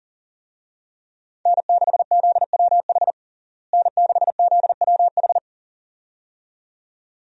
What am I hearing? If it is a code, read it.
Morse code, heard as a keyed tone, about 40 wpm, 700 Hz. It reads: N6ZWH N6ZWH
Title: USA Call Signs - 5 Characters - Repeated 2x - 40wpm